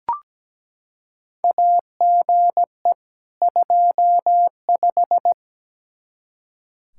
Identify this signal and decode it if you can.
Morse code, heard as a keyed tone, about 17 wpm, 700 Hz. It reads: AGE 25
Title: Practice QSO - Exchange 3 - More Info - 17wpm